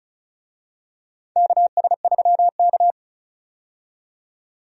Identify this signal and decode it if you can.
Morse code, heard as a keyed tone, about 36 wpm, 700 Hz. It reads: KS3K
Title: USA Call Signs - 4 Characters - 35wpm